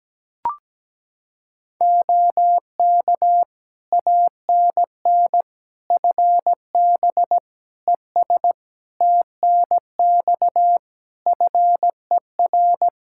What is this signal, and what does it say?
Morse code, heard as a keyed tone, about 17 wpm, 700 Hz. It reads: OK ANN FB ES TNX FER
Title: K1USN SST QSO Practice 17wpm